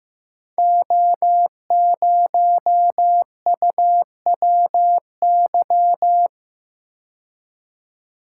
Morse code, heard as a keyed tone, about 15 wpm, 700 Hz. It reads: O0UWY
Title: Code Groups - 5 Letter:Numbers - 15wpm